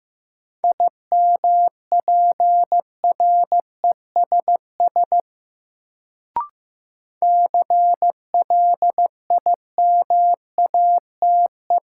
Morse code, heard as a keyed tone, about 15 wpm, 700 Hz. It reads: IMPRESS CLIMATE
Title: Sentences from Top 1600 Words - New Word Drill Alt - 15wpm